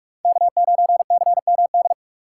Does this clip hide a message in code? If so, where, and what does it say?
Yes: Morse code from 0.2 to 1.9 s, heard as a keyed tone, about 42 wpm, 700 Hz. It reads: K9CMD